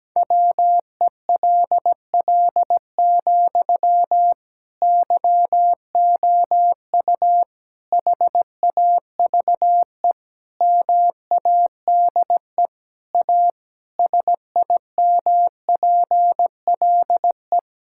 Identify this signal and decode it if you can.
Morse code, heard as a keyed tone, about 17 wpm, 700 Hz. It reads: WELL, YOU HAVE MADE A SIMPLE
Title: Sentences from Top 1000 Words - Review Part 6 - 17wpm